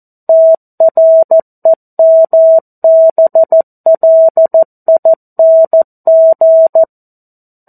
Morse code, heard as a keyed tone, about 14 wpm, 650 Hz. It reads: TREMBLING
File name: 02-Chapter-02 14 WPM